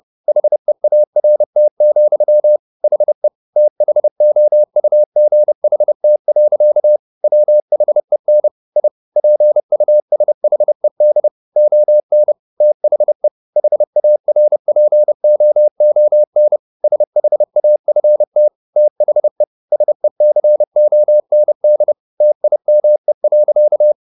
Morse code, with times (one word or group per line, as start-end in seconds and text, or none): 0.3-2.6 HEART,
2.8-3.3 HE
3.6-7.0 THOUGHT.
7.2-8.5 WHEN
8.8-8.9 I
9.2-11.3 PUSHED
11.6-12.3 ON
12.6-13.3 THE
13.6-16.6 HARPOON
16.8-18.5 SHAFT
18.8-19.4 THE
19.7-21.9 SECOND
22.2-23.9 TIME.